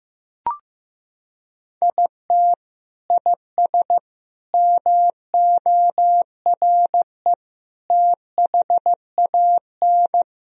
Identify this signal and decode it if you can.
Morse code, heard as a keyed tone, about 15 wpm, 700 Hz. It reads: IT IS MORE THAN
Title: Sentences from Top 1000 Words - Review Part 4 - 15wpm